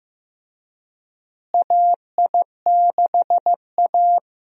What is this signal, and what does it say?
Morse code, heard as a keyed tone, about 15 wpm, 700 Hz. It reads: AI6A
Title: USA Call Signs - 4 Characters - Repeated 2x - 15wpm